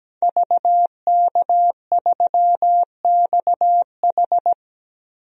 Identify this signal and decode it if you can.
Morse code, heard as a keyed tone, about 17 wpm, 700 Hz. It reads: VK3XH
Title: DXCC-100 Call Signs - 5 Characters - Repeated 3x - 17wpm